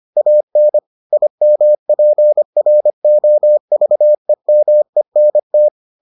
Morse code, heard as a keyed tone, about 25 wpm, 600 Hz. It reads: AN IMPROVEMENT